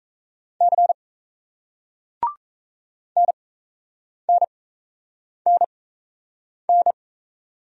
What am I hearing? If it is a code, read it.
Morse code, heard as a keyed tone, about 38 wpm, spaced as for 8 wpm, 700 Hz. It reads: C NNNN